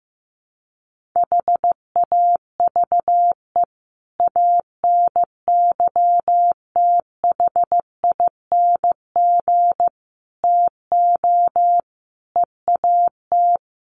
Morse code, heard as a keyed tone, about 15 wpm, 700 Hz. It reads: HAVE ANYTHING TO EAT
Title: Sentences from Top 1500 Words 15wpm